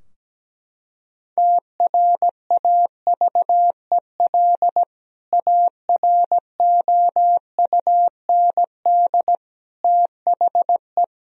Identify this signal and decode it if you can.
Morse code, heard as a keyed tone, about 17 wpm, 700 Hz. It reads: TRAVEL AROUND THE
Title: Sentences from Top 1000 Words - Review Part 3 - 17wpm